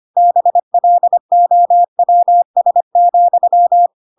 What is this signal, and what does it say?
Morse code, heard as a keyed tone, about 25 wpm, 700 Hz. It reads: BLOWS,
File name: Pinocchio / Pinocchio ch2 25 25 ik1hge